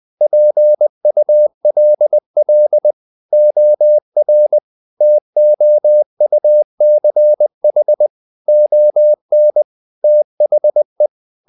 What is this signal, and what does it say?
Morse code, heard as a keyed tone, about 20 wpm, 600 Hz. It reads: PULL OR TOUCH ON THE
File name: the old man and the sea / the old man and the sea part 1 20wpm